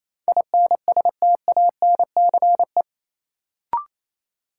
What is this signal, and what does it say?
Morse code, heard as a keyed tone, about 28 wpm, 700 Hz. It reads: INSTANCE
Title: Sentences from Top 800 Words - New Word Drill - 28wpm